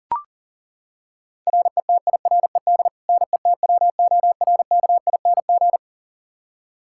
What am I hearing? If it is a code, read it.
Morse code, heard as a keyed tone, about 40 wpm, 700 Hz. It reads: RETIRED NETWORKING
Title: K1USN SST QSO Practice 40wpm